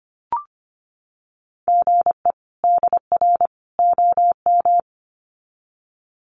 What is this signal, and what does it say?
Morse code, heard as a keyed tone, about 25 wpm, 700 Hz. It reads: GE DR OM
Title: Common QSO Phrases 25wpm